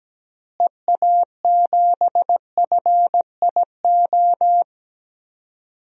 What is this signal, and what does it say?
Morse code, heard as a keyed tone, about 17 wpm, 700 Hz. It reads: EA7FIO
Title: DXCC-100 Call Signs - 6 Characters - Repeated 2x - 17wpm